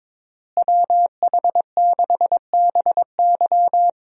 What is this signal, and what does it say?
Morse code, heard as a keyed tone, about 22 wpm, 700 Hz. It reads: WH6BY
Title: USA Call Signs - 5 Characters - Repeated 2x - 22wpm